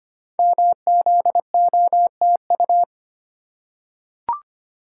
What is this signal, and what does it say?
Morse code, heard as a keyed tone, about 25 wpm, 690 Hz. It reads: MZOTU
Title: Code Groups - 5 Letters - 25wpm